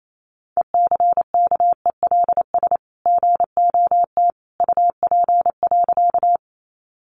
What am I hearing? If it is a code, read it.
Morse code, heard as a keyed tone, about 28 wpm, 700 Hz. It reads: ECKELS GOT UP.
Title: A Sound Of Thunder 28wpm